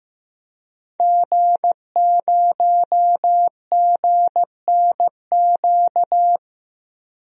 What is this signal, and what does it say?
Morse code, heard as a keyed tone, about 15 wpm, 700 Hz. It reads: G0GNQ
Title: DXCC-100 Call Signs - 5 Characters - 15wpm